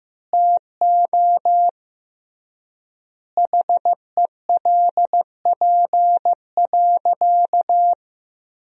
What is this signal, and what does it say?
Morse code, heard as a keyed tone, about 15 wpm, 700 Hz. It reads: TO HELP.
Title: 3x Word-Spacing - Sentences from Top 300 Words - 15wpm